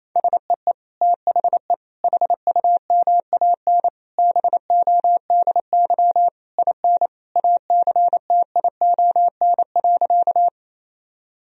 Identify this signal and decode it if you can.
Morse code, heard as a keyed tone, about 28 wpm, 700 Hz. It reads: SEE THE HUMAN BODY IN ACTION.
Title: Sentences from Top 1000 Words - Review Part 4 - 28wpm